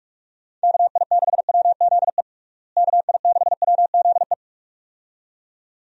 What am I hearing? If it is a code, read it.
Morse code, heard as a keyed tone, about 46 wpm, 700 Hz. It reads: KI6WZE KI6WZE
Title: USA Call Signs - 6 Characters - Repeated 2x - 45wpm